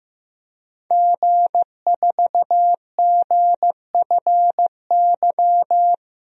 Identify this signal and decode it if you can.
Morse code, heard as a keyed tone, about 15 wpm, 700 Hz. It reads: G4GFY
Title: DXCC-100 Call Signs - 5 Characters - 15wpm